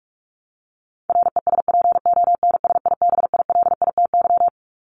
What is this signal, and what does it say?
Morse code, heard as a keyed tone, about 44 wpm, 700 Hz. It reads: RESPONSIBILITY
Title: Top 1001-1250 Words 45wpm